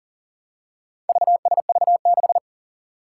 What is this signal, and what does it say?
Morse code, heard as a keyed tone, about 40 wpm, 700 Hz. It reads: VSV6
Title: Four Challenging Letter-Number Combinations 40wpm